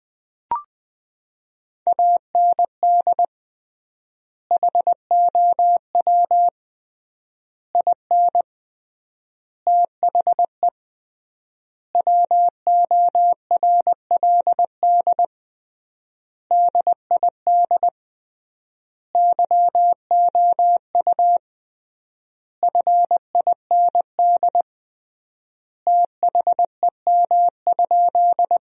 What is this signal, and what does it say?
Morse code, heard as a keyed tone, about 21 wpm, 700 Hz. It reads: AND HOW IN THE WORLD DID YOU FIND THEM?
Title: 3x Word-Spacing - Sentences from Top 200 Words - 20wpm